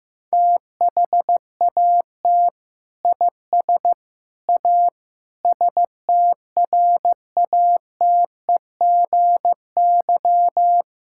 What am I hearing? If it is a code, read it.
Morse code, heard as a keyed tone, about 15 wpm, 700 Hz. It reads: THAT IS A STRATEGY
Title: Sentences from Top 1000 Words - Review Part 5 - 15wpm